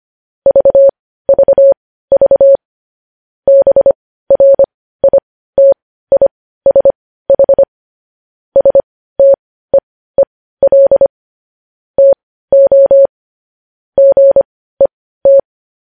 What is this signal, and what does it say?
Morse code, heard as a keyed tone, about 25 wpm, spaced as for 15 wpm, 550 Hz. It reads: VVV BRITISH STEEL TO GET
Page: news headlines in morse code at 15 wpm